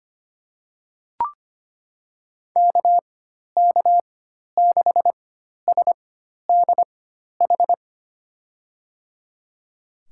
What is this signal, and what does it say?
Morse code, heard as a keyed tone, about 25 wpm, spaced as for 12 wpm, 700 Hz. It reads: KK6SDH
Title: USA Call Signs - 6 Characters - 4x Character Spacing - 25wpm